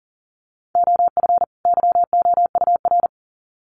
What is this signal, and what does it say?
Morse code, heard as a keyed tone, about 40 wpm, 700 Hz. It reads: OF YOUR
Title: Sets of 2 Words 40wpm